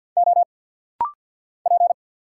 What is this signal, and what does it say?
Morse code, heard as a keyed tone, about 49 wpm, 700 Hz. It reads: O P
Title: Single Letter-Number Alt - Rapid-Fire 50wpm